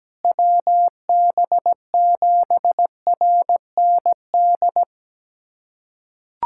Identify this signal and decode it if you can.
Morse code, heard as a keyed tone, about 17 wpm, 700 Hz. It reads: WB7RND
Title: USA Callsign DE Callsign - 6 Characters - 17wpm